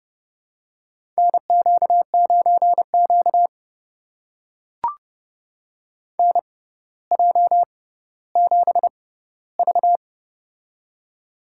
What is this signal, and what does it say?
Morse code, heard as a keyed tone, about 30 wpm, spaced as for 15 wpm, 700 Hz. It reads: NQ9Q NJ7V
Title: USA Call Signs - 4 Characters - 6x Character Spacing - 30wpm